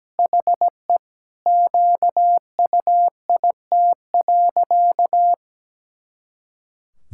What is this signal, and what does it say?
Morse code, heard as a keyed tone, about 17 wpm, 700 Hz. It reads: HE QUIT.
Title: Pangram Challenge v2 - 17wpm